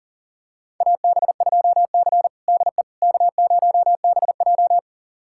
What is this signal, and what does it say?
Morse code, heard as a keyed tone, about 40 wpm, 700 Hz. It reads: AB2C DE K0BJ
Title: USA Callsign DE Callsign - 4 Characters - 40wpm